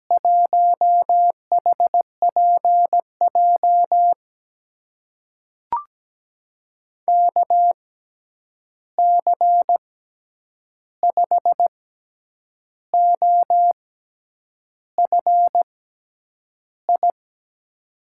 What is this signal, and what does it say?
Morse code, heard as a keyed tone, about 17 wpm, spaced as for 8 wpm, 700 Hz. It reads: 1HPJ KC5OFI
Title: USA Call Signs - 6 Characters - 6x Character Spacing - 17wpm